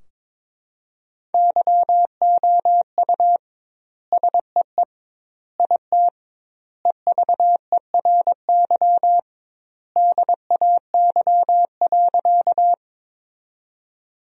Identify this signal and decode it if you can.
Morse code, heard as a keyed tone, about 21 wpm, 700 Hz. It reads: YOU SEE IT EVERY DAY.
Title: 2x Word-Spacing - Sentences from Top 200 Words - 22wpm